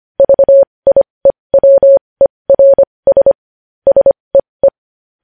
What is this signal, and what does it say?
Morse code, heard as a keyed tone, about 25 wpm, spaced as for 20 wpm, 550 Hz. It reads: VIEWERS SEE
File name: podcast / News headlines-1x-25x20